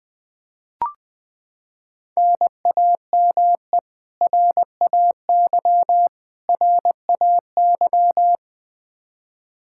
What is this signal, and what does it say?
Morse code, heard as a keyed tone, about 20 wpm, 700 Hz. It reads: NAME RAY RAY
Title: Practice QSO - Exchange 1 - Beginning - 20wpm